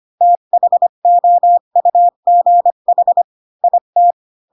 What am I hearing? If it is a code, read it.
Morse code, heard as a keyed tone, about 25 wpm, 700 Hz. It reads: THOUGH IT
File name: Pinocchio / Pinocchio ch5 20 25 ik1hge